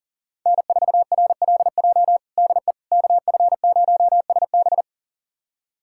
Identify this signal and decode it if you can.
Morse code, heard as a keyed tone, about 40 wpm, 700 Hz. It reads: N4RLJ DE KF0SB
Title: USA Callsign DE Callsign - 5 Characters - 40wpm